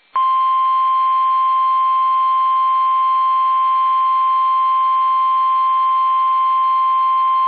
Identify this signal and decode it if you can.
Emergency Alert System (SAME) tones (the 1050 Hz attention tone). It is heard from 0.3 to 6.7 s.